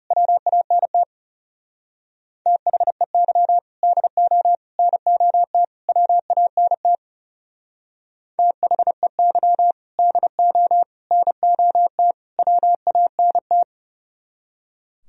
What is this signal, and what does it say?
Morse code, heard as a keyed tone, about 34 wpm, 690 Hz. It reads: WANT THEY DO NOT WANT THEY DO NOT WANT
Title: Speed-Racing - Sets of 4 Words 30wpm